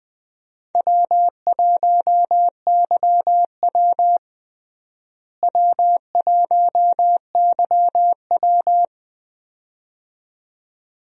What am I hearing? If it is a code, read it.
Morse code, heard as a keyed tone, about 21 wpm, 700 Hz. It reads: W1YW W1YW
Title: USA Call Signs - 4 Characters - Repeated 2x - 20wpm